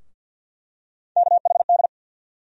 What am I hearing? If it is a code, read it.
Morse code, heard as a keyed tone, about 52 wpm, 700 Hz. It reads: KHD
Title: USA Call Sign Postfixes - 3 Letters 50wpm